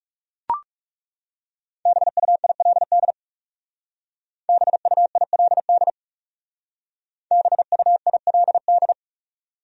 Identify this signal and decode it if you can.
Morse code, heard as a keyed tone, about 38 wpm, 700 Hz. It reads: BUILD BUILD BUILD